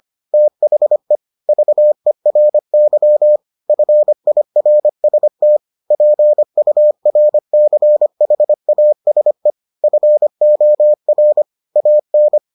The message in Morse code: THE VERY FIRST PURCHASE FOR AN

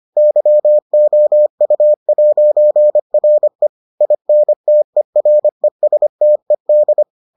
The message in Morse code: YOU'RE INTERESTED

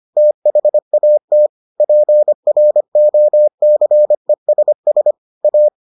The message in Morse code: THAT PROCESS A

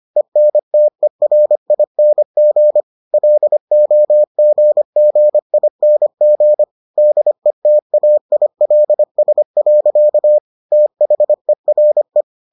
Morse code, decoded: ENTERING LOGGING DETAILS. THERE

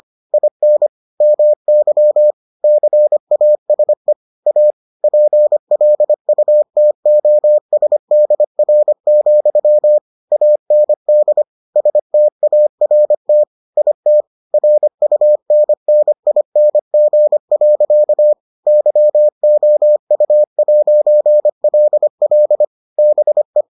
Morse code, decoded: IN MY CASE A PLUTOSDR, AND START IT RUNNING. YOU'LL BE

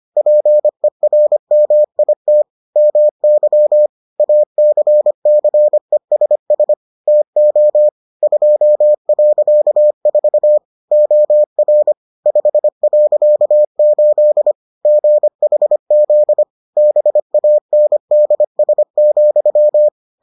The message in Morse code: PERMIT MY ACCESS TO 2.4 OR 5.8 GHZ BANDS,

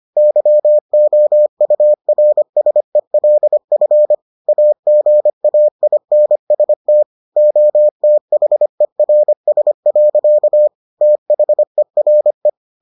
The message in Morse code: YOURSELF AGAINST OTHERS. THERE

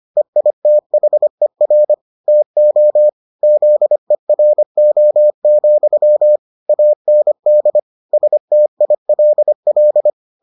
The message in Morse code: EITHER TO ZERO, AND STILL

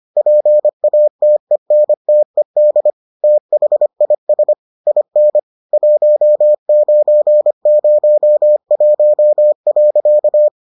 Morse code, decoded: PATENTED THIS IN 1901.

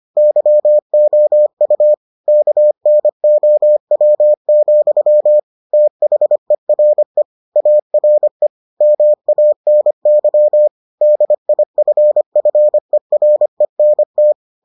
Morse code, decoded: YOU KNOW, THERE ARE MANY DIFFERENT

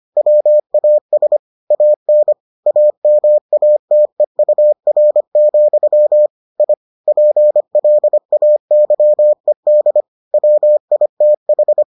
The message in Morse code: WAS AN AMATEUR, I PLAYED WITH

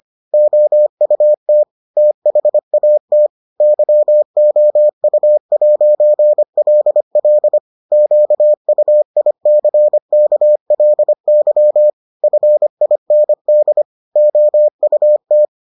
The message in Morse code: OUT THAT YOU'LL QUICKLY FIND OUT

